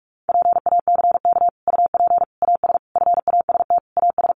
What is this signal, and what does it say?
Morse code, heard as a keyed tone, about 44 wpm, 700 Hz. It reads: PACK UP AS FAST AS
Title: Sentences from Top 1300 Words 45wpm